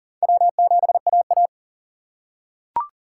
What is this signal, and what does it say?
Morse code, heard as a keyed tone, about 40 wpm, 700 Hz. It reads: W7AA
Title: USA Call Signs - 6x Character Spacing - 40wpm